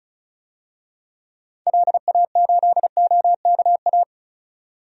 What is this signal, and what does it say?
Morse code, heard as a keyed tone, about 34 wpm, 700 Hz. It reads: LA8OKA
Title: DXCC-100 Call Signs - 6 Characters - 35wpm